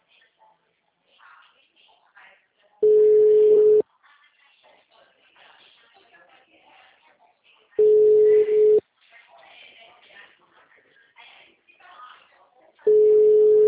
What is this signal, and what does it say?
Ringback tone (Europe, 425 Hz), with speech.
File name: out- -2013- - - .410